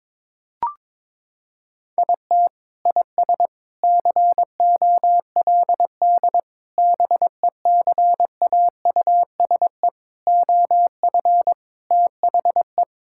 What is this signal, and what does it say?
Morse code, heard as a keyed tone, about 22 wpm, 700 Hz. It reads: IT IS COLD BECAUSE OF THE
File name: practice / 22wpm / Sentences with 3-Word Phrases - 22wpm